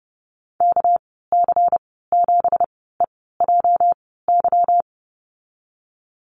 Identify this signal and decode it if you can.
Morse code, heard as a keyed tone, about 30 wpm, spaced as for 17 wpm, 700 Hz. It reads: KC7EJY
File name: Lesson 37g - Letter x - Call Signs 30wpm